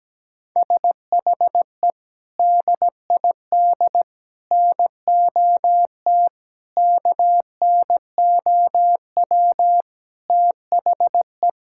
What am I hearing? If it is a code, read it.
Morse code, heard as a keyed tone, about 17 wpm, 700 Hz. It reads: SHE DID NOT KNOW THE
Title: Sentences from Top 1600 Words 17wpm